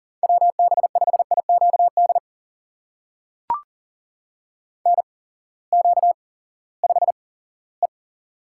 Morse code, heard as a keyed tone, about 40 wpm, spaced as for 20 wpm, 700 Hz. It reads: WB5IQD NQ5E
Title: USA Call Signs - 8x Character Spacing - 40wpm